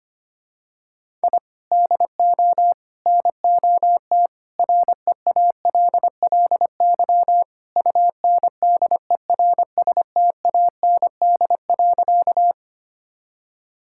Morse code, heard as a keyed tone, about 25 wpm, 700 Hz. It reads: I DO NOT REALLY UNDERSTAND.
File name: practice / 25wpm / Sentences with 4-Word Phrases - 25wpm